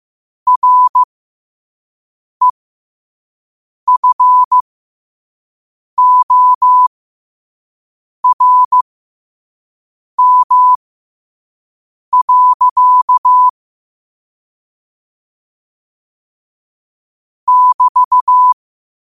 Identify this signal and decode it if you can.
Morse code, heard as a keyed tone, about 15 wpm, spaced as for 5 wpm, 1000 Hz. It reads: REFORM. <BT>
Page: New quotes every day in morse code at 5 Words per minute.